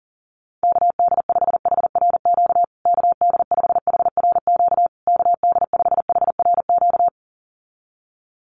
Morse code, heard as a keyed tone, about 40 wpm, 700 Hz. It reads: KD5HRQ KD5HRQ KD5HRQ
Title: USA Call Signs Repeated 3x 40wpm